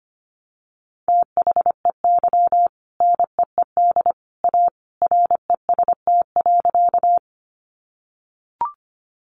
Morse code, heard as a keyed tone, about 25 wpm, 700 Hz. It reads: THEY NEED A REST.
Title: Sentences from Top 500 Words 25wpm